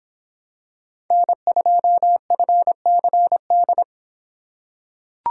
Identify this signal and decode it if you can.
Morse code, heard as a keyed tone, about 26 wpm, 700 Hz. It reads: N2FCD